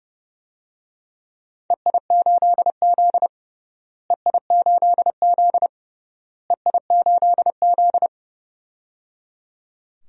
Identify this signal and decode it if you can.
Morse code, heard as a keyed tone, about 30 wpm, 700 Hz. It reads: EI8Z EI8Z EI8Z
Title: DXCC-100 Call Signs - 4 Characters - Repeated 3x - 30wpm